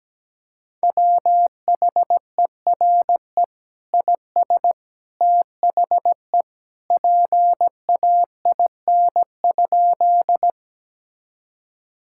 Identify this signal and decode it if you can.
Morse code, heard as a keyed tone, about 17 wpm, 700 Hz. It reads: WHERE IS THE PAIN?
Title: Sentences from Top 1600 Words 17wpm